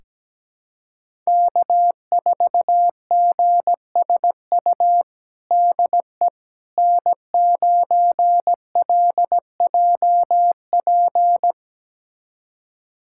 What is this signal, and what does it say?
Morse code, heard as a keyed tone, about 17 wpm, 700 Hz. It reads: K4GSU DE N9LJP
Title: USA Callsign DE Callsign 17wpm